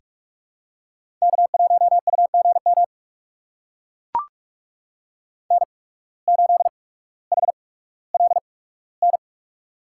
Morse code, heard as a keyed tone, about 44 wpm, spaced as for 19 wpm, 700 Hz. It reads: K1UGM N8HLN
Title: USA Call Signs - 5 Characters - 8x Character Spacing - 45wpm